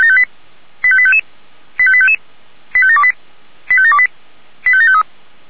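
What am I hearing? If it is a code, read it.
Five-tone selective call (ZVEI), heard as quick runs of five tones, tones of about 70 ms. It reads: 85855 85855 85728 85728 85753
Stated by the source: ZVEI five-tone selective call 85723, received on SDR.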